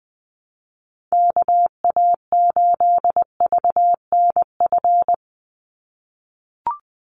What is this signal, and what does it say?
Morse code, heard as a keyed tone, about 20 wpm, 700 Hz. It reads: KA8VNF
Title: USA Call Signs 20wpm